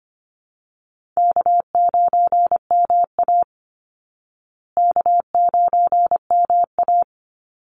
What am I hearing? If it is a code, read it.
Morse code, heard as a keyed tone, about 25 wpm, 700 Hz. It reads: K9MA K9MA
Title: Call Signs Repeated 2x - 25wpm